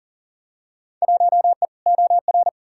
Morse code, heard as a keyed tone, about 40 wpm, 700 Hz. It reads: 1E OR
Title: Field Day Exchanges - 40wpm